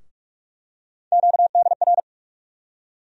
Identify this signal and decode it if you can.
Morse code, heard as a keyed tone, about 46 wpm, 700 Hz. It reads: QDR
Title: USA Call Sign Postfixes - 3 Letters 45wpm